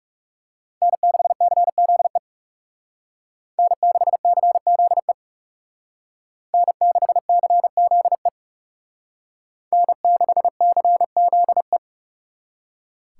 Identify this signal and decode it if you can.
Morse code, heard as a keyed tone, about 38 wpm, 700 Hz. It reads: N6CZE N6CZE N6CZE N6CZE